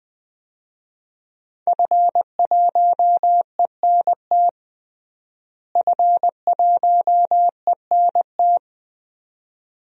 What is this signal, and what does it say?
Morse code, heard as a keyed tone, about 21 wpm, 700 Hz. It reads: F1ENT F1ENT
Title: DXCC-100 Call Signs - 5 Characters - Repeated 2x - 20wpm